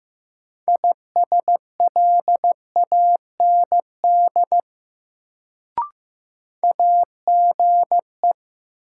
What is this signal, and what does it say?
Morse code, heard as a keyed tone, about 15 wpm, 700 Hz. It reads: ISLAND AGE
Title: Sentences from Top 500 Words - New Word Drill - 15wpm